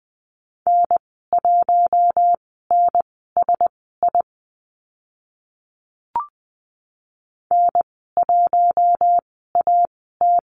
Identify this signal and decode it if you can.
Morse code, heard as a keyed tone, about 20 wpm, spaced as for 14 wpm, 700 Hz. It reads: N1NSI N1AT
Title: 042 Lesson 8g - Number 1 - Call Signs 20wpm